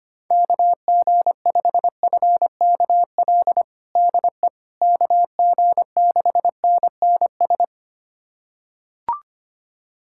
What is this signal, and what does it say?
Morse code, heard as a keyed tone, about 25 wpm, 700 Hz. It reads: KG5FKL DE KG6NNS
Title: USA Callsign DE Callsign 25wpm